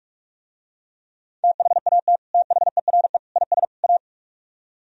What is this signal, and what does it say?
Morse code, heard as a keyed tone, about 45 wpm, 700 Hz. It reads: THAT THERE IS A